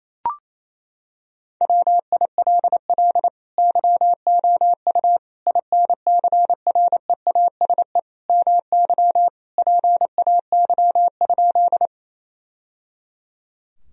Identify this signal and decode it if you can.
Morse code, heard as a keyed tone, about 28 wpm, 700 Hz. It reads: WILL YOU INCREASE MY PAY?
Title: Sentences from Top 1000 Words - Review Part 4 - 28wpm